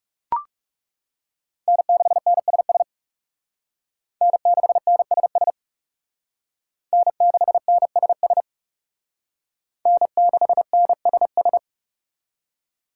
Morse code, heard as a keyed tone, about 38 wpm, 700 Hz. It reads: N6NSS N6NSS N6NSS N6NSS